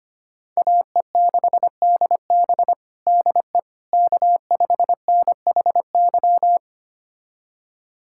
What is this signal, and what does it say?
Morse code, heard as a keyed tone, about 25 wpm, 700 Hz. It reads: AE6DB DE K5NHY
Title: USA Callsign DE Callsign - 5 Characters - 25wpm